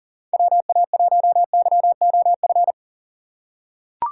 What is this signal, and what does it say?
Morse code, heard as a keyed tone, about 41 wpm, 700 Hz. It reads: WA1YOF
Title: USA Call Signs - 6 Characters - 6x Character Spacing - 40wpm